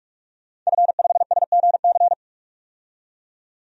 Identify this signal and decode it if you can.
Morse code, heard as a keyed tone, about 44 wpm, 690 Hz. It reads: F5SGC
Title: DXCC-100 Call Signs - 5 Characters - 45wpm